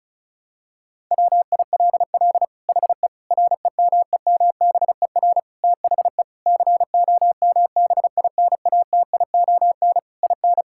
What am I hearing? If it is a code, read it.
Morse code, heard as a keyed tone, about 34 wpm, 700 Hz. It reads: WILL HE REMEMBER THE COMBINATION IN
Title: Sentences from Top 1000 Words - Review Part 6 - 35wpm